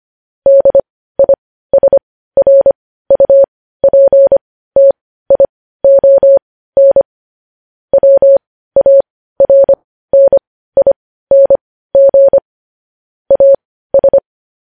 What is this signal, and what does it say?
Morse code, heard as a keyed tone, about 25 wpm, spaced as for 15 wpm, 550 Hz. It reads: DISRUPTION WARNING AS
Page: News Headlines in Morse Code at 15 WPM